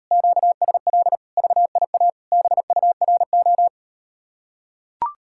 Morse code, heard as a keyed tone, about 38 wpm, 700 Hz. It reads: QSL VIA BURO